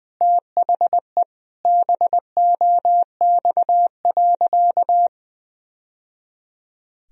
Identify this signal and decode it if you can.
Morse code, heard as a keyed tone, about 21 wpm, 700 Hz. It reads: THE BOX.
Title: Sentences from Top 1000 Words - Review Part 4 - 20wpm